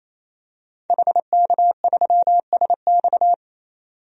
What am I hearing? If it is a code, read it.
Morse code, heard as a keyed tone, about 28 wpm, 700 Hz. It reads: HK3SX
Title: DXCC-100 Call Signs - 5 Characters - 28wpm